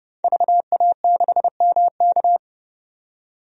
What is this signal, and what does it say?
Morse code, heard as a keyed tone, about 32 wpm, 700 Hz. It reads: VA6MK
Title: DXCC-100 Call Signs - 5 Characters - 30wpm